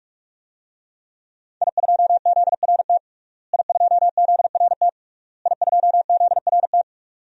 Morse code, heard as a keyed tone, about 46 wpm, 700 Hz. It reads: I2ZRT I2ZRT I2ZRT
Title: DXCC-100 Call Signs - 5 Characters - Repeated 3x - 45wpm